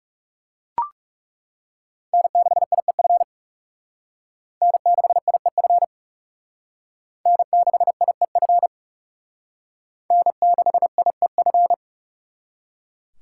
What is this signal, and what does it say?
Morse code, heard as a keyed tone, about 37 wpm, 700 Hz. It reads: N6IEF N6IEF N6IEF N6IEF
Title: Speed Racing - USA Call Signs 30wpm